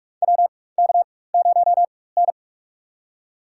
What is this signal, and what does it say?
Morse code, heard as a keyed tone, about 44 wpm, spaced as for 21 wpm, 700 Hz. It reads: WK0N
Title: USA Call Signs - 4 Characters - 4x Character Spacing - 45wpm